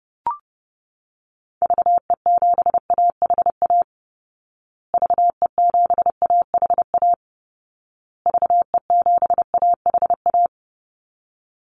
Morse code, heard as a keyed tone, about 30 wpm, 700 Hz. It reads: VE7AHA VE7AHA VE7AHA
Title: Call Signs Repeated 2x - 30wpm